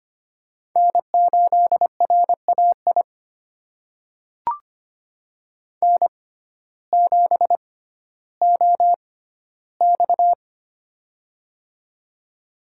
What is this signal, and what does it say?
Morse code, heard as a keyed tone, about 25 wpm, spaced as for 14 wpm, 700 Hz. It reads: N8RAI N7OX
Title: USA Call Signs - 6x Character Spacing - 25wpm